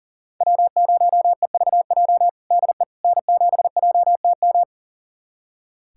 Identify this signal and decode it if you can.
Morse code, heard as a keyed tone, about 40 wpm, 700 Hz. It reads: W0EVJ DE N7JTM
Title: USA Callsign DE Callsign - 5 Characters - 40wpm